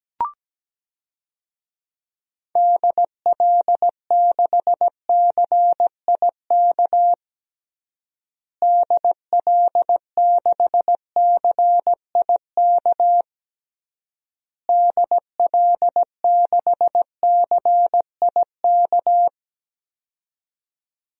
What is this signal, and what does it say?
Morse code, heard as a keyed tone, about 17 wpm, 700 Hz. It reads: DL6CIK DL6CIK DL6CIK
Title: DXCC-100 Call Signs - 6 Characters - Repeated 3x - 17wpm